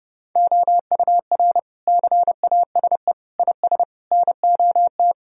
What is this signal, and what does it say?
Morse code, heard as a keyed tone, about 32 wpm, 700 Hz. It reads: OUR CASE IS NOT
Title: Sentences from Top 1000 Words - Review Part 2 - 30wpm